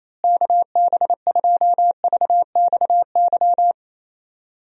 Morse code, heard as a keyed tone, about 27 wpm, 700 Hz. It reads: KB2VXY
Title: USA Call Signs - 6 Characters - Repeated 3x - 28wpm